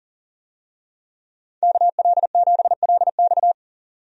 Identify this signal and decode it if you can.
Morse code, heard as a keyed tone, about 38 wpm, 700 Hz. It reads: KL7LX
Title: DXCC-100 Call Signs - 5 Characters - 40wpm